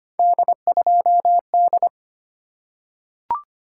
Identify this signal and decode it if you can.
Morse code, heard as a keyed tone, about 25 wpm, 700 Hz. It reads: D2D
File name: Three Letter-Number Combinations - 25wpm